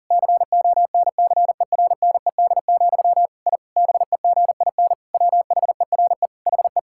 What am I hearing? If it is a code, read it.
Morse code, heard as a keyed tone, about 40 wpm, 700 Hz. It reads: CONCERNED, I BEGIN WHERE HE
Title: Sentences from Top 1000 Words - Review Part 6 - 40wpm